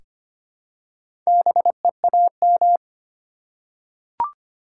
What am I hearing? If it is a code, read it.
Morse code, heard as a keyed tone, about 25 wpm, 700 Hz. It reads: BEAM
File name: Top 90 QSO Elements - 8x Character Spacing - 25wpm